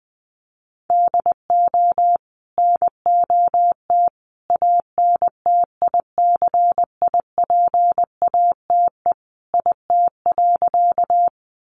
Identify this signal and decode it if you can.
Morse code, heard as a keyed tone, about 20 wpm, 700 Hz. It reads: DO NOT ANTICIPATE IT.
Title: Sentences from Top 1500 Words 20wpm